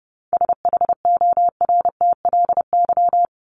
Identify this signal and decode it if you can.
Morse code, heard as a keyed tone, about 28 wpm, 700 Hz. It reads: SHORTLY
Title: LY Suffix - Top 100 Words with Base Words 30wpm